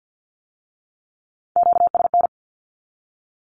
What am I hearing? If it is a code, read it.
Morse code, heard as a keyed tone, about 50 wpm, 700 Hz. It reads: QSN
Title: Q-Codes with Spoken Meaning - 50wpm